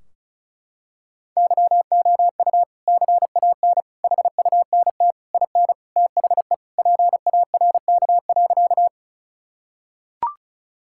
Morse code, heard as a keyed tone, about 34 wpm, 700 Hz. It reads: YOU CAN HUNT IN THE PARK.
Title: Sentences from Top 1600 Words 35wpm